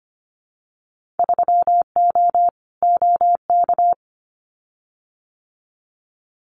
Morse code, heard as a keyed tone, about 25 wpm, 700 Hz. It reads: 3O OK
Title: Winter Field Day Exchanges 25wpm